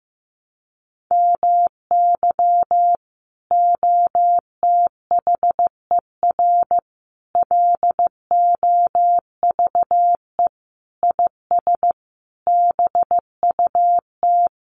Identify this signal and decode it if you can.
Morse code, heard as a keyed tone, about 14 wpm, 700 Hz. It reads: MY OTHER LOVE IS BUT
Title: Sentences from Top 1300 Words 15wpm